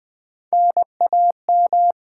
Morse code, heard as a keyed tone, about 20 wpm, 700 Hz. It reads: NAM
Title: Top 90 QSO Elements - 8x Character Spacing - 20wpm